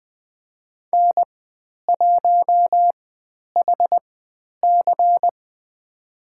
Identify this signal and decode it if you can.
Morse code, heard as a keyed tone, about 20 wpm, spaced as for 10 wpm, 700 Hz. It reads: N1HC
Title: 095 Lesson 16g - Letter c - Call Signs 20wpm